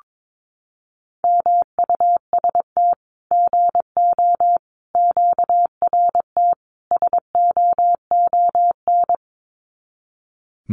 Morse code, heard as a keyed tone, about 22 wpm, 700 Hz. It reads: MUST GO QRT SOON
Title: Common QSO Phrases 22wpm